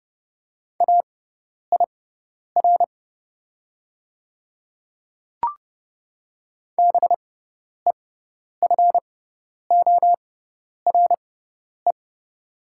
Morse code, heard as a keyed tone, about 30 wpm, spaced as for 10 wpm, 700 Hz. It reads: AIR BEFORE
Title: 6x Character Spacing - Top 200 Words - 30wpm